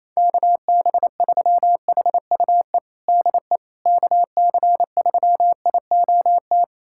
Morse code, heard as a keyed tone, about 28 wpm, 700 Hz. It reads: KB3HUE DE KC3IOT
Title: USA Callsign DE Callsign - 6 Characters - 28wpm